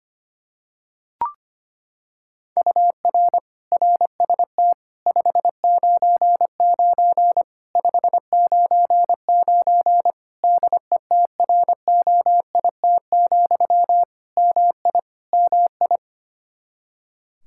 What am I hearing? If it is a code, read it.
Morse code, heard as a keyed tone, about 25 wpm, 700 Hz. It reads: UR RST 599 599 DETROIT, MI MI
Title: Practice QSO - Exchange 1 - Beginning - 25wpm